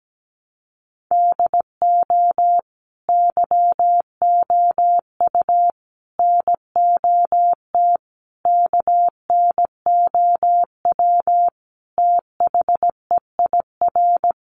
Morse code, heard as a keyed tone, about 17 wpm, 700 Hz. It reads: DO YOU NOT KNOW THEIR
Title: Sentences from Top 600 Words 17wpm